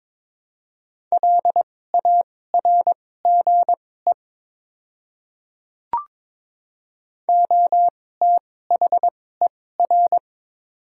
Morse code, heard as a keyed tone, about 23 wpm, spaced as for 16 wpm, 700 Hz. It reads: LARGE OTHER
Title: 2x Character Spacing - Top 300 Words - 22wpm